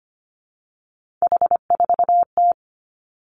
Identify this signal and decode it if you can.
Morse code, heard as a keyed tone, about 25 wpm, 700 Hz. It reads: H4T
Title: Three Challenging Letter-Number Combinations 25wpm